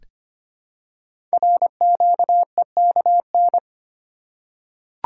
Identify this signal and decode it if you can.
Morse code, heard as a keyed tone, about 25 wpm, 700 Hz. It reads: RQEKN
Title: Code Groups - 5 Letters - 25wpm